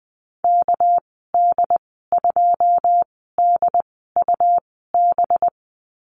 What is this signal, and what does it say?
Morse code, heard as a keyed tone, about 20 wpm, spaced as for 15 wpm, 700 Hz. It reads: KD2DUB
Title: 238 Lesson 38g - Letter q - Call Signs 20wpm